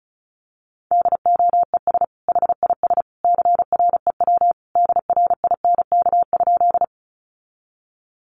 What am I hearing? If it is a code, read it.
Morse code, heard as a keyed tone, about 34 wpm, 700 Hz. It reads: DOES HIS CREW DRINK?
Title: Sentences from Top 1500 Words 35wpm